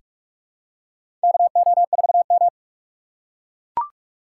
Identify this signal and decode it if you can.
Morse code, heard as a keyed tone, about 44 wpm, 700 Hz. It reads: KO4M
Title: USA Call Signs - 4 Characters - 2x Character Spacing - 45wpm